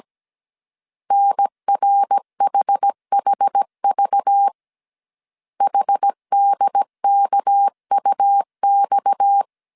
Morse code, heard as a keyed tone, about 17 wpm, 800 Hz. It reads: NRHHV HDKUX
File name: CW / DET / 1 6 1500C S A 16wpm 2635